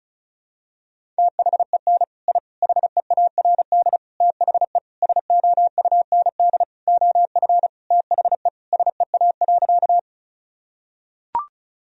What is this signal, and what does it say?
Morse code, heard as a keyed tone, about 36 wpm, 700 Hz. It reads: THEN I HEARD THE SOUND OF THE SEA.
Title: Sentences from Top 1000 Words - Review Part 1 - 35wpm